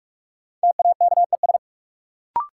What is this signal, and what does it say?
Morse code, heard as a keyed tone, about 46 wpm, 700 Hz. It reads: TAKES
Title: Sets of 5 Words Encore - New Word Drill - 45wpm